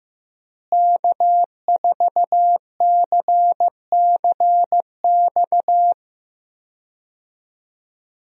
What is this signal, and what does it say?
Morse code, heard as a keyed tone, about 15 wpm, 700 Hz. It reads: K4CCX
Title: USA Call Signs - 5 Characters - Repeated 3x - 15wpm